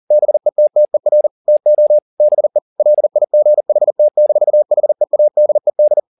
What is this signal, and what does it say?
Morse code, heard as a keyed tone, about 40 wpm, 600 Hz. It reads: BETTER TO BE LIGHT-HEADED
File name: the old man and the sea 2 40wpm